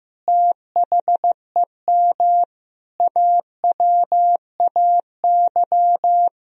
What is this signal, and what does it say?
Morse code, heard as a keyed tone, about 15 wpm, 700 Hz. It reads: THEM AWAY
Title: Sentences from Top 1000 Words - Review Part 2 - 15wpm